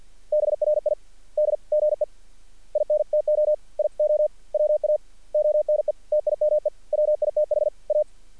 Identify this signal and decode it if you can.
Morse code, heard as a keyed tone, about 48 wpm, spaced as for 41 wpm, 600 Hz. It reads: BRIDGE INTO IOWA ONE TIME WITH A
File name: archived / CRSnet